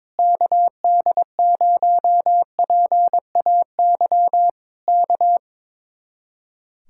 Morse code, heard as a keyed tone, about 22 wpm, 700 Hz. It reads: KD0PAY K
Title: Speed-Racing Common QSO Phrases - 22wpm